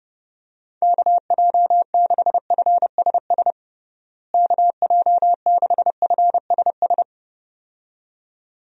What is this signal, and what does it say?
Morse code, heard as a keyed tone, about 30 wpm, 700 Hz. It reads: KJ6FSS KJ6FSS
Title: USA Call Signs - 6 Characters - Repeated 2x - 30wpm